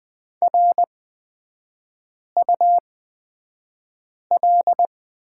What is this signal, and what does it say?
Morse code, heard as a keyed tone, about 20 wpm, spaced as for 5 wpm, 690 Hz. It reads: RUL
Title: 255 Lesson 41e - Period - Three Characters 20wpm